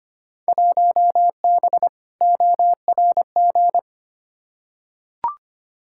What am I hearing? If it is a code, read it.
Morse code, heard as a keyed tone, about 25 wpm, 700 Hz. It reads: 1B ORG
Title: Field Day Exchanges Encore - 25wpm